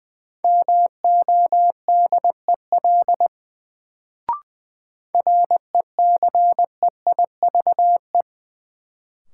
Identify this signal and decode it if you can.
Morse code, heard as a keyed tone, about 20 wpm, 700 Hz. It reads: MODEL RECEIVE
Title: Sentences from Top 1000 Words - New Word Drill - 20wpm